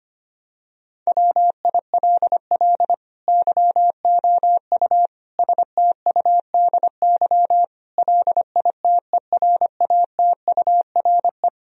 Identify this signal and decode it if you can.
Morse code, heard as a keyed tone, about 25 wpm, 700 Hz. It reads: WILL YOU STUDY LITERATURE
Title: Sentences from Top 1000 Words - Review Part 3 - 25wpm